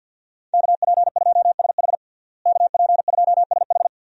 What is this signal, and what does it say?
Morse code, heard as a keyed tone, about 48 wpm, 700 Hz. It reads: KP2SH KP2SH
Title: DXCC-100 Call Signs - 5 Characters - Repeated 2x - 50wpm